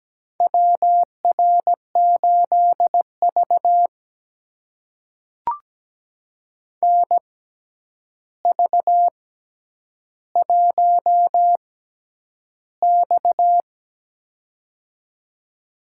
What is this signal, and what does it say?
Morse code, heard as a keyed tone, about 17 wpm, spaced as for 9 wpm, 700 Hz. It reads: WR8V NV1X
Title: USA Call Signs - 4 Characters - 6x Character Spacing - 17wpm